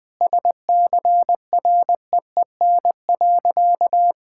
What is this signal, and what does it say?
Morse code, heard as a keyed tone, about 21 wpm, 700 Hz. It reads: SCREEN.
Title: Sentences from Top 1000 Words - Review Part 6 - 20wpm